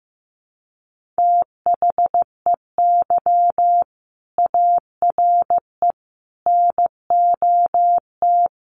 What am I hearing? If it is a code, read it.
Morse code, heard as a keyed tone, about 14 wpm, 700 Hz. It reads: THEY ARE NOT
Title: Sentences from Top 1200 Words 15wpm